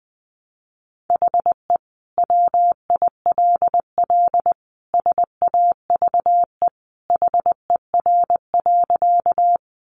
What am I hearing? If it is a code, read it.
Morse code, heard as a keyed tone, about 20 wpm, 700 Hz. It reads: HE WILL SAVE HER.
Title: Sentences from Top 1100 Words 20wpm